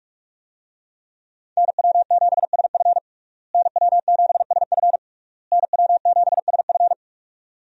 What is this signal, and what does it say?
Morse code, heard as a keyed tone, about 46 wpm, 700 Hz. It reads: NW7SF NW7SF NW7SF
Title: USA Call Signs - 5 Characters - Repeated 3x - 45wpm